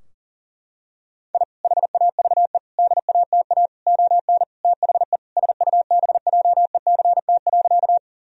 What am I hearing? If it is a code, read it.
Morse code, heard as a keyed tone, about 40 wpm, 700 Hz. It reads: I HAVE DATA ON THE SUBJECT.
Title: Sentences from Top 1000 Words - Review Part 3 - 40wpm